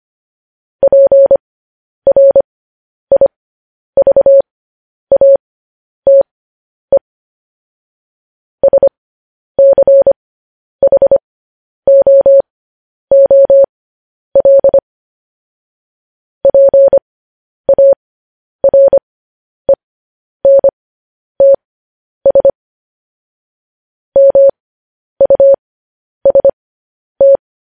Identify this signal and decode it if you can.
Morse code, heard as a keyed tone, about 25 wpm, spaced as for 10 wpm, 550 Hz. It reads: PRIVATE SCHOOL PARENTS MUST
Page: news headlines in morse code at 10 wpm